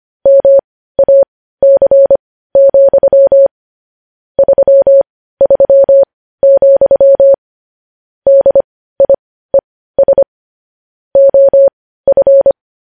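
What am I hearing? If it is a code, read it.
Morse code, heard as a keyed tone, about 25 wpm, spaced as for 15 wpm, 550 Hz. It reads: MAC, 33, DIES OF
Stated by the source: News headlines sent as Morse practice at 15.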